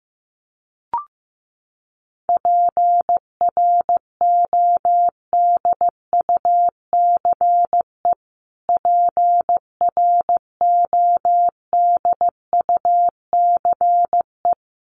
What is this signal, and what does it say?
Morse code, heard as a keyed tone, about 15 wpm, 700 Hz. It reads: PRODUCE PRODUCE
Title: Top 500 Words Repeated 15wpm